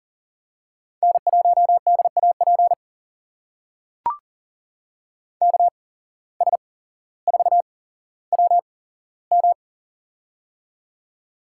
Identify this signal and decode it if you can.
Morse code, heard as a keyed tone, about 38 wpm, spaced as for 17 wpm, 700 Hz. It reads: N1DAP KS4WM
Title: USA Call Signs - 5 Characters - 8x Character Spacing - 40wpm